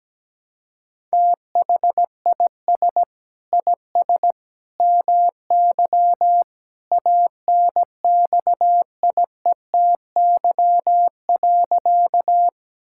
Morse code, heard as a keyed tone, about 17 wpm, 700 Hz. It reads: THIS IS MY ANXIETY.
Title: Sentences from Top 1600 Words 17wpm